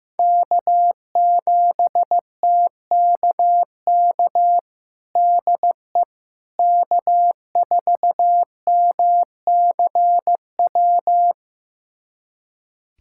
Morse code, heard as a keyed tone, about 15 wpm, 700 Hz. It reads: K7TKK DE K4MCW
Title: USA Callsign DE Callsign - 5 Characters - 15wpm